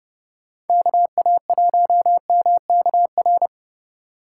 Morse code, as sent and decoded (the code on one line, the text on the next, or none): -.- .- .---- -- -.- .-.
KA1MKR